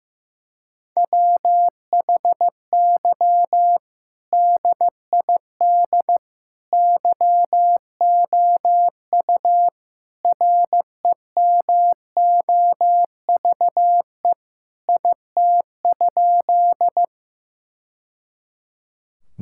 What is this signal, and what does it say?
Morse code, heard as a keyed tone, about 15 wpm, 700 Hz. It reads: WHY DID YOU REMOVE IT?
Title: Sentences from Top 1000 Words - Review Part 5 - 15wpm